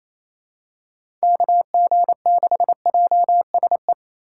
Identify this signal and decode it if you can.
Morse code, heard as a keyed tone, about 28 wpm, 700 Hz. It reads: KG6JSE
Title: USA Call Signs - 6 Characters - 2x Character Spacing - 28wpm